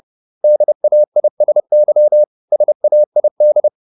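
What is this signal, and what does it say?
Morse code, heard as a keyed tone, about 30 wpm, 600 Hz. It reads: DAISY SAID
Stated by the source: Morse code at 30 wpm, the novel Never by Ken Follett in English, from CW med Gnister.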